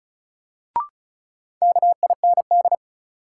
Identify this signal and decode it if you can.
Morse code, heard as a keyed tone, about 36 wpm, 700 Hz. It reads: KIND
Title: Sentences from Top 200 Words - New Word Drill - 35wpm